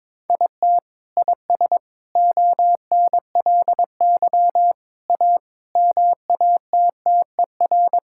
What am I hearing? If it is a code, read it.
Morse code, heard as a keyed tone, about 22 wpm, 700 Hz. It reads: IT IS ONLY A MATTER
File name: practice / 22wpm / Sentences with 8-Word Phrases - 22wpm